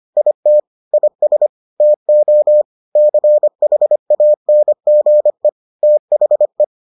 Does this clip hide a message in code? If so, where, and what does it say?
Yes: Morse code from 0.2 to 6.6 s, heard as a keyed tone, about 25 wpm, 600 Hz. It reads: IT IS TO CHANGE THE